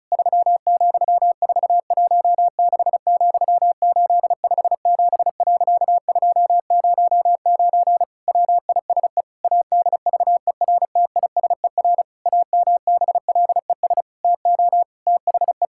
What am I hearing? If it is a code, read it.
Morse code, heard as a keyed tone, about 34 wpm, 700 Hz. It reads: 3,416,857.209 WISE ADVERTISER AMBLES TO THE
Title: Pangram Challenge v2 - 35wpm